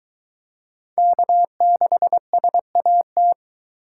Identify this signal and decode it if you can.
Morse code, heard as a keyed tone, about 22 wpm, 700 Hz. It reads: K6SAT